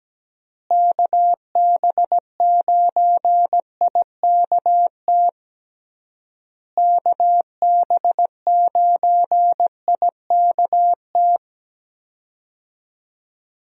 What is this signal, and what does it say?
Morse code, heard as a keyed tone, about 17 wpm, 700 Hz. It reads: KB9IKT KB9IKT
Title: USA Call Signs - 6 Characters - Repeated 3x - 17wpm